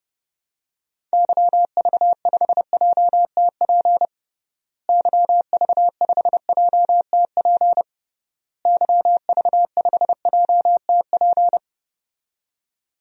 Morse code, heard as a keyed tone, about 30 wpm, 700 Hz. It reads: YV5JTP YV5JTP YV5JTP
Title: DXCC-100 Call Signs - 6 Characters - Repeated 3x - 30wpm